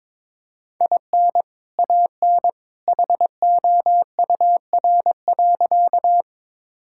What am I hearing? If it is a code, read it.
Morse code, heard as a keyed tone, about 22 wpm, 700 Hz. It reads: IN AN HOUR.
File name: practice / 22wpm / Sentences with 2-Word Phrases - 22wpm